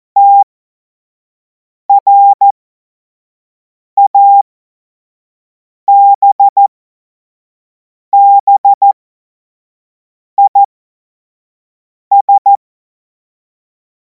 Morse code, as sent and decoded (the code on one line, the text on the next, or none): - .-. .- -... -... .. ...
TRABBIS